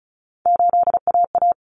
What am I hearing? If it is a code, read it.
Morse code, heard as a keyed tone, about 34 wpm, 700 Hz. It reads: 8AA
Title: Common Cut Numbers - 3-Digit 35wpm